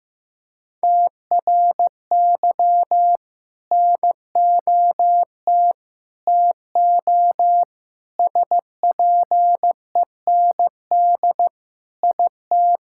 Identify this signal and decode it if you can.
Morse code, heard as a keyed tone, about 15 wpm, 700 Hz. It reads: TRY NOT TO SPEND IT
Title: Sentences from Top 1000 Words - Review Part 5 - 15wpm